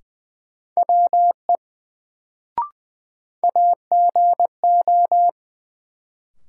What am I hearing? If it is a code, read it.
Morse code, heard as a keyed tone, about 20 wpm, 700 Hz. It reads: WE AGO
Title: Sets of 2 Words - New Word Drill - 20wpm